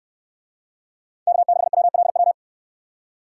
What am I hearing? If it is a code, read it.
Morse code, heard as a keyed tone, about 69 wpm, 700 Hz. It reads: K6WLW